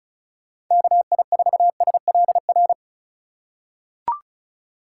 Morse code, heard as a keyed tone, about 36 wpm, 700 Hz. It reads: KI4SLR
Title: USA Call Signs - 6 Characters - 2x Character Spacing - 35wpm